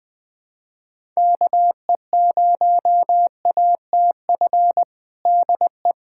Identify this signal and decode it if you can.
Morse code, heard as a keyed tone, about 20 wpm, 700 Hz. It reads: KE0ATF DE
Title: USA Callsign DE Callsign 20wpm